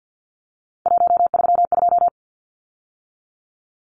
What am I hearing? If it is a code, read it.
Morse code, heard as a keyed tone, about 50 wpm, 700 Hz. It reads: 132
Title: Numbers - 3-Digit 50wpm